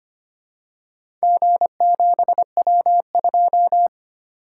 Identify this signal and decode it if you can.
Morse code, heard as a keyed tone, about 25 wpm, 700 Hz. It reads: G7W2
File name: practice / 25wpm / Four Letter-Number Combinations - 25wpm